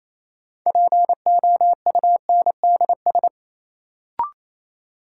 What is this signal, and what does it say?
Morse code, heard as a keyed tone, about 28 wpm, 700 Hz. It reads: POUNDS
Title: Sentences from Top 1000 Words - Review Part 2 - New Word Drill - 28wpm